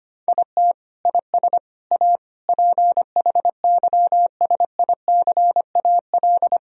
Morse code, heard as a keyed tone, about 25 wpm, 700 Hz. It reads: IT IS A PHYSICAL
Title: Sentences from Top 1000 Words - Review Part 4 - 25wpm